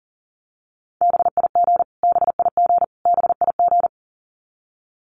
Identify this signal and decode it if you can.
Morse code, heard as a keyed tone, about 40 wpm, 700 Hz. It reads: BIG BIG BIG
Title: Top 200 Words Repeated 40wpm